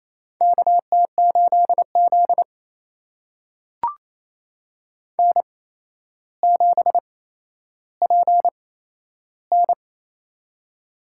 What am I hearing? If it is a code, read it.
Morse code, heard as a keyed tone, about 28 wpm, spaced as for 12 wpm, 700 Hz. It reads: KT8Z N7PN
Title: USA Call Signs - 4 Characters - 8x Character Spacing - 28wpm